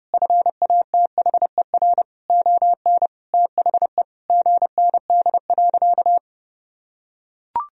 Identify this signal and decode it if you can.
Morse code, heard as a keyed tone, about 30 wpm, 700 Hz. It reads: FATHER ON THE GND.
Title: Sentences from Top 500 Words and Abbreviations 30wpm